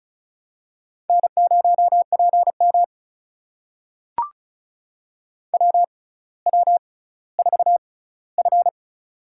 Morse code, heard as a keyed tone, about 36 wpm, spaced as for 18 wpm, 700 Hz. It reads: N0PM WW4F
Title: USA Call Signs - 4 Characters - 6x Character Spacing - 35wpm